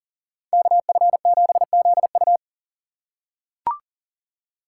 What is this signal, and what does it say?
Morse code, heard as a keyed tone, about 40 wpm, 700 Hz. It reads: KF7ZU
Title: USA Call Signs - 5 Characters - 2x Character Spacing - 40wpm